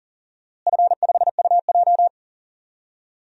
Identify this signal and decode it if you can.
Morse code, heard as a keyed tone, about 39 wpm, 690 Hz. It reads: F5UJ
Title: DXCC-100 Call Signs - 4 Characters - Repeated 2x - 40wpm